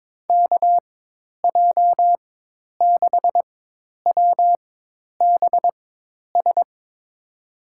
Morse code, heard as a keyed tone, about 22 wpm, spaced as for 10 wpm, 700 Hz. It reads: KJ6WBS
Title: USA Call Signs - 4x Character Spacing - 22wpm